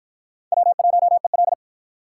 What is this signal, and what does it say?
Morse code, heard as a keyed tone, about 53 wpm, 700 Hz. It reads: W1EL